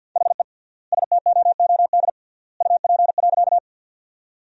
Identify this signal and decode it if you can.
Morse code, heard as a keyed tone, about 50 wpm, 700 Hz. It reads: HE STOOD UP.
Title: 3x Word-Spacing - Sentences from Top 500 Words - 50wpm